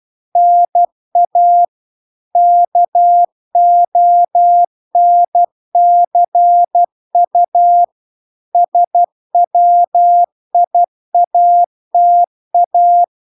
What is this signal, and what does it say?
Morse code, heard as a keyed tone, about 12 wpm, 700 Hz. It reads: NA KONCU SWIATA